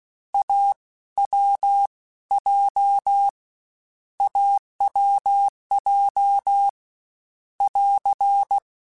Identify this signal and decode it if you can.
Morse code, heard as a keyed tone, about 16 wpm, spaced as for 13 wpm, 800 Hz. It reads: AWJ AWJ <AR>
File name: Short Runs / Short Runs-mp3s / AWJ @15